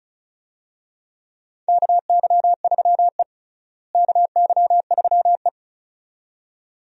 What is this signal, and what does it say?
Morse code, heard as a keyed tone, about 36 wpm, 700 Hz. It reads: KY3E KY3E
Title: USA Call Signs - 4 Characters - Repeated 2x - 35wpm